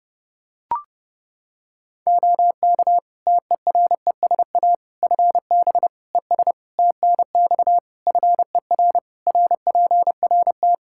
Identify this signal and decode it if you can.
Morse code, heard as a keyed tone, about 30 wpm, 700 Hz. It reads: OK TERESA FB ES TNX FER RPRT
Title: Practice QSO - Exchange 2 - Info - 30wpm